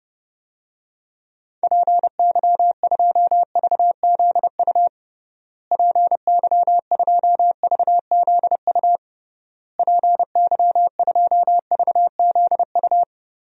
Morse code, heard as a keyed tone, about 31 wpm, 700 Hz. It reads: PY2VZU PY2VZU PY2VZU
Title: DXCC-100 Call Signs - 6 Characters - Repeated 3x - 30wpm